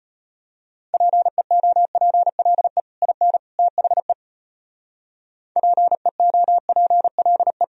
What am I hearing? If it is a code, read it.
Morse code, heard as a keyed tone, about 31 wpm, 700 Hz. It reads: PEOPLE IN THE PEOPLE
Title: Speed-Racing - Sets of 3 Words 25wpm